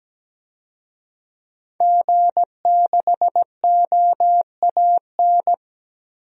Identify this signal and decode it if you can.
Morse code, heard as a keyed tone, about 17 wpm, 700 Hz. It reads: G6OAN
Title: DXCC-100 Call Signs - 5 Characters - 17wpm